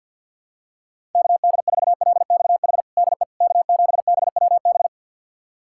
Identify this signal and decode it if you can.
Morse code, heard as a keyed tone, about 49 wpm, 700 Hz. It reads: KD4LXH DE K7BWB
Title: USA Callsign DE Callsign 50wpm